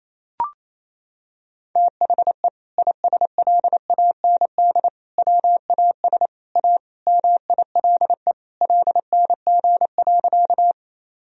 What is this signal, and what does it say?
Morse code, heard as a keyed tone, about 28 wpm, 700 Hz. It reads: THE ISLAND WAS A MILE LNG.
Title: Sentences from Top 500 Words and Abbreviations 28wpm